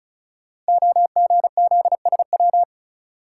Morse code, heard as a keyed tone, about 35 wpm, 690 Hz. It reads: OGZSW
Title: Code Groups - 5 Letter:Numbers - 35wpm